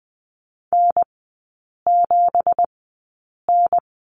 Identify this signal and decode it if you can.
Morse code, heard as a keyed tone, about 20 wpm, spaced as for 8 wpm, 690 Hz. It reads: N7N
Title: 205 Lesson 33e - Letter j - Three Characters 20wpm